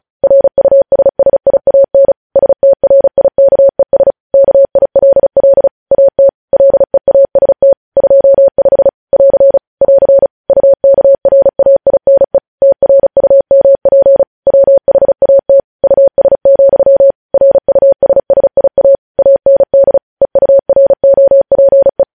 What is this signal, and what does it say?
Morse code, heard as a keyed tone, about 36 wpm, 550 Hz. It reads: RUSSIAN STRIKES KILL AT LEAST 25 <AR> <AR> UKRAINE TRUMP WHAT US, RUSSIA AND EUROPE
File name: podcast / News headlines-1x-35x35